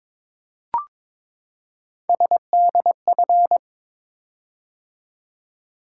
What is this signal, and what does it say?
Morse code, heard as a keyed tone, about 22 wpm, 700 Hz. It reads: SDF
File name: practice / 22wpm / Three Letter-Number Combinations - 22wpm